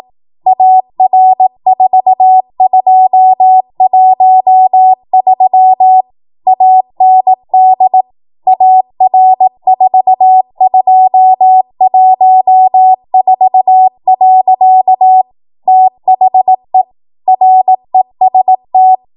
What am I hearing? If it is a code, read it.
Morse code, heard as a keyed tone, about 18 wpm, 750 Hz. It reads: AR4213 AND AR4214. THE REST
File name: Morse / 18WPM